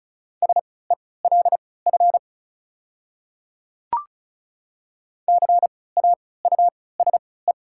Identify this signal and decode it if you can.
Morse code, heard as a keyed tone, about 34 wpm, spaced as for 20 wpm, 700 Hz. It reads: SELF CAUSE
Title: 3x Character Spacing - Top 300 Words - 35wpm